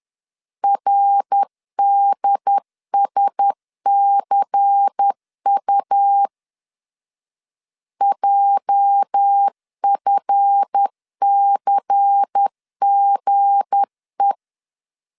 Morse code, heard as a keyed tone, about 10 wpm, 800 Hz. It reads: RDSCU JFCGE
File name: LEZIONI / 8 YJ45 / 8 YJ45 10wpm 2549